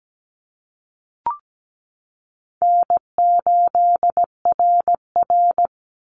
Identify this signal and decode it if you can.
Morse code, heard as a keyed tone, about 17 wpm, 700 Hz. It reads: N8RR
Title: Easy Call Signs Repeated 2x - 17wpm